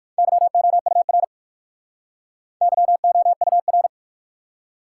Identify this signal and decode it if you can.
Morse code, heard as a keyed tone, about 58 wpm, 700 Hz. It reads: YOUR YOUR